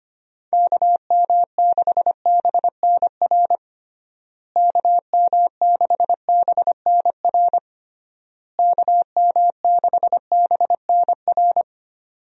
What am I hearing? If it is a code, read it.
Morse code, heard as a keyed tone, about 25 wpm, 700 Hz. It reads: KM6BNR KM6BNR KM6BNR
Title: USA Call Signs - 6 Characters - Repeated 3x - 25wpm